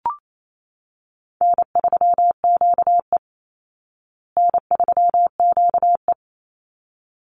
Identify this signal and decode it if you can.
Morse code, heard as a keyed tone, about 28 wpm, 700 Hz. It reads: N3QE N3QE
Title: Easy Call Signs Repeated 2x - 28wpm